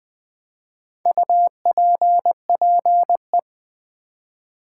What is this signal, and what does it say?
Morse code, heard as a keyed tone, about 20 wpm, 700 Hz. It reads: UPPE
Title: Build A Word - 1 Letter At A Time - 3 to 5-Letter Words - 20wpm